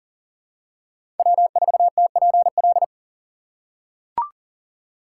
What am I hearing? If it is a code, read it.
Morse code, heard as a keyed tone, about 40 wpm, 690 Hz. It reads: W4TPL
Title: USA Call Signs - 5 Characters - 3x Character Spacing - 40wpm